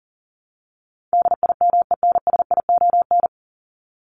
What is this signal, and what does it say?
Morse code, heard as a keyed tone, about 40 wpm, 700 Hz. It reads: DIMENSION
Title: Top 1501-1750 Words 40wpm